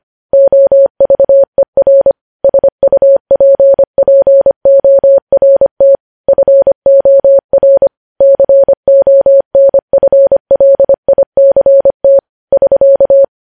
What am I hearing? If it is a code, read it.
Morse code, heard as a keyed tone, about 25 wpm, 550 Hz. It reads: OVER SUPPORT FOR CONFLICT <SK>
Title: News Headlines at 25